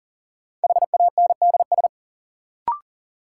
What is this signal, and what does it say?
Morse code, heard as a keyed tone, about 40 wpm, 700 Hz. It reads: HANDS
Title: Sets of 4 Words - New Word Drill - 40wpm